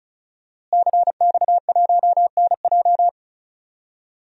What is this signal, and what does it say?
Morse code, heard as a keyed tone, about 36 wpm, 700 Hz. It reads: CX1NJ
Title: DXCC-100 Call Signs - 5 Characters - Repeated 2x - 35wpm